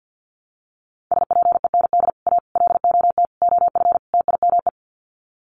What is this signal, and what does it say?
Morse code, heard as a keyed tone, about 50 wpm, 700 Hz. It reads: SPEND A LOT OF TIME
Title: Sets of 5 Words 50wpm